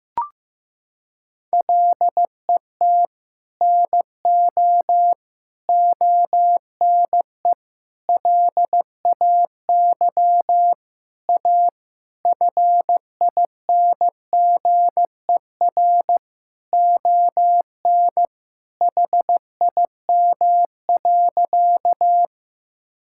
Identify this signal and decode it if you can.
Morse code, heard as a keyed tone, about 16 wpm, 700 Hz. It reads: LET NO ONE LAY A FINGER ON HIM.
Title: Sentences from Top 1600 Words 15wpm